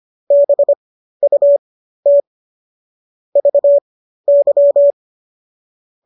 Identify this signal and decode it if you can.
Morse code, heard as a keyed tone, about 25 wpm, spaced as for 13 wpm, 570 Hz. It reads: BUT VY